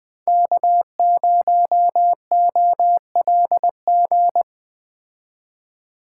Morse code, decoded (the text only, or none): K0OLG